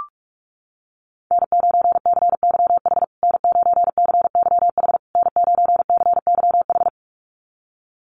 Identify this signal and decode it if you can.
Morse code, heard as a keyed tone, about 44 wpm, 690 Hz. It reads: N9CYH N9CYH N9CYH
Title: USA Call Signs Repeated 3x 45wpm